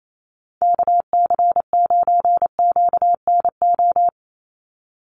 Morse code, heard as a keyed tone, about 28 wpm, 700 Hz. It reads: KC9QNO
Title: USA Call Signs 28wpm